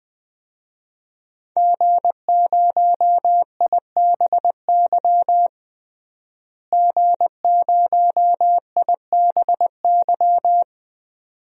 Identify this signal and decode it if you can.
Morse code, heard as a keyed tone, about 20 wpm, 700 Hz. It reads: G0IBY G0IBY
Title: DXCC-100 Call Signs - 5 Characters - Repeated 2x - 20wpm